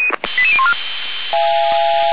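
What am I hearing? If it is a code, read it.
Five-tone selective call (ZVEI), heard as quick runs of five tones, tones of about 70 ms. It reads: 88014
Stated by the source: ZVEI five-tone selective call received by SDR.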